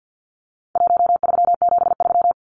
Morse code, heard as a keyed tone, about 50 wpm, 700 Hz. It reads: 1373
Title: Numbers - Contest Style - 4-Digit 50wpm